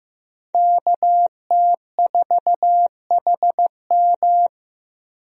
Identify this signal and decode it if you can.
Morse code, heard as a keyed tone, about 16 wpm, 700 Hz. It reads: KT4HM
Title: USA Call Signs - 5 Characters - Repeated 2x - 15wpm